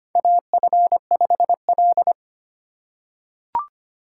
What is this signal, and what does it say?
Morse code, heard as a keyed tone, about 25 wpm, 700 Hz. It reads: AF5L
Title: USA Call Signs - 3x Character Spacing - 25wpm